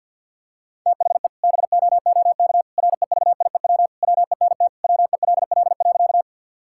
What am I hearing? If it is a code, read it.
Morse code, heard as a keyed tone, about 49 wpm, 700 Hz. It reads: THE BOOK REVIEW WENT WELL.
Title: Sentences from Top 1000 Words - Review Part 6 - 50wpm